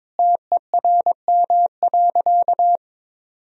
Morse code, heard as a keyed tone, about 22 wpm, 700 Hz. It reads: TERM.
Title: Sentences from Top 1000 Words - Review Part 4 - 22wpm